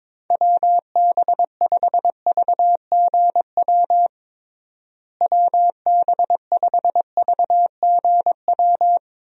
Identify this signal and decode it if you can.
Morse code, heard as a keyed tone, about 22 wpm, 700 Hz. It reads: WB5VGW WB5VGW
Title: USA Call Signs - 6 Characters - Repeated 3x - 22wpm